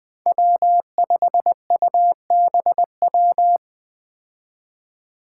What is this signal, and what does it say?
Morse code, heard as a keyed tone, about 21 wpm, 700 Hz. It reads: W5UBW
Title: Five Challenging Letter-Number Combinations 20wpm